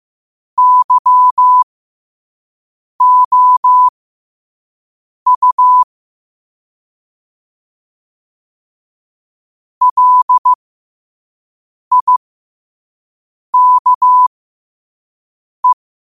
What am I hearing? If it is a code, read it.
Morse code, heard as a keyed tone, about 15 wpm, spaced as for 5 wpm, 1000 Hz. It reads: YOU LIKE
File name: qotdfiles / 5wpm